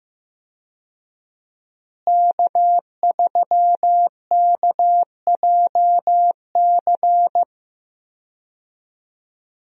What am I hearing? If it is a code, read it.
Morse code, heard as a keyed tone, about 15 wpm, 700 Hz. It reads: K3KJC
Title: USA Call Signs - 5 Characters - Repeated 2x - 15wpm